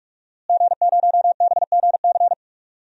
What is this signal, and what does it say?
Morse code, heard as a keyed tone, about 44 wpm, 700 Hz. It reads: G0BGC
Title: DXCC-100 Call Signs - 5 Characters - Repeated 2x - 45wpm